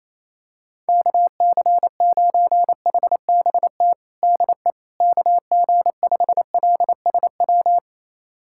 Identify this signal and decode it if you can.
Morse code, heard as a keyed tone, about 28 wpm, 700 Hz. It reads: KC9HBT DE KG5LSW
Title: USA Callsign DE Callsign 28wpm